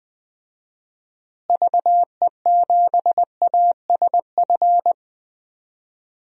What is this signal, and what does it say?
Morse code, heard as a keyed tone, about 21 wpm, 700 Hz. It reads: VE7ASF
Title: DXCC-100 Call Signs - 6 Characters - Repeated 3x - 20wpm